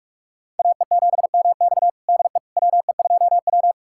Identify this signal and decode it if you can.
Morse code, heard as a keyed tone, about 46 wpm, 700 Hz. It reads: AE7MX DE WE2W
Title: USA Callsign DE Callsign 45wpm